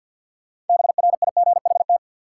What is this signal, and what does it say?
Morse code, heard as a keyed tone, about 53 wpm, 700 Hz. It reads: BRIGHT
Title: Sentences from Top 1000 Words - New Word Drill - 50wpm